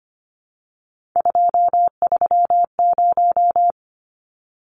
Morse code, heard as a keyed tone, about 25 wpm, 700 Hz. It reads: 230
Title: Numbers - 3-Digit 25wpm